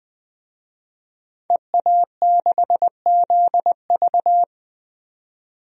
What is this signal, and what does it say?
Morse code, heard as a keyed tone, about 21 wpm, 700 Hz. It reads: EA6ZV
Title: DXCC-100 Call Signs - 5 Characters - 20wpm